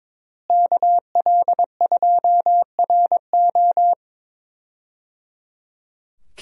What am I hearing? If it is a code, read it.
Morse code, heard as a keyed tone, about 22 wpm, 700 Hz. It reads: KL2RO
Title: USA Call Signs - 5 Characters - Repeated 3x - 22wpm